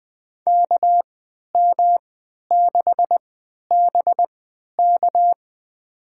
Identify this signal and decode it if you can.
Morse code, heard as a keyed tone, about 20 wpm, spaced as for 11 wpm, 700 Hz. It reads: KM6BK
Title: USA Call Signs - 3x Character Spacing - 20wpm